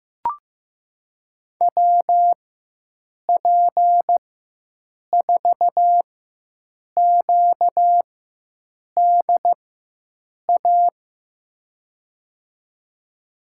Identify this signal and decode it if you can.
Morse code, heard as a keyed tone, about 15 wpm, spaced as for 7 wpm, 700 Hz. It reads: WP4QDA
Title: USA Call Signs - 6 Characters - 4x Character Spacing - 15wpm